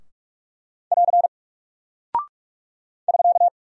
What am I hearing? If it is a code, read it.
Morse code, heard as a keyed tone, about 44 wpm, 700 Hz. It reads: <AR> <SK>
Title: Single Character (Letter:Number:Punctuation:Prosign) 45wpm